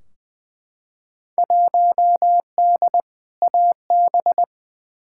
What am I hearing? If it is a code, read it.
Morse code, heard as a keyed tone, about 21 wpm, 700 Hz. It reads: 1D AB
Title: Field Day Exchanges Encore - 20wpm